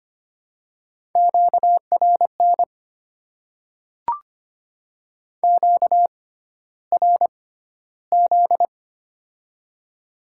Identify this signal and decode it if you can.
Morse code, heard as a keyed tone, about 25 wpm, spaced as for 13 wpm, 690 Hz. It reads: QRN QRZ
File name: Top 90 QSO Elements - 6x Character Spacing - 25wpm